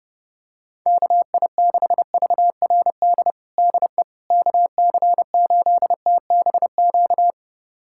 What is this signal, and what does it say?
Morse code, heard as a keyed tone, about 30 wpm, 700 Hz. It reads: KI6VRD DE KC8TBQ
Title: USA Callsign DE Callsign 30wpm